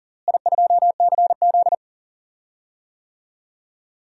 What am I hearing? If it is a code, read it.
Morse code, heard as a keyed tone, about 40 wpm, 700 Hz. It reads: I2CZ
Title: DXCC-100 Call Signs - 4 Characters - Repeated 2x - 40wpm